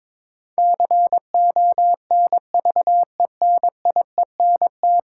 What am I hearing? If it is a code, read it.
Morse code, heard as a keyed tone, about 22 wpm, 700 Hz. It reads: CONVENIENT
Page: morse-code-ninja-practice-22wpm